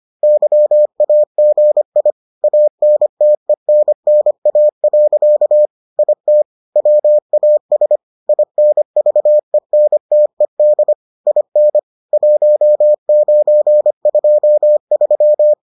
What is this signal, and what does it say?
Morse code, heard as a keyed tone, about 25 wpm, 600 Hz. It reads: YAGI ANTENNA. IT WAS INVENTED IN 1923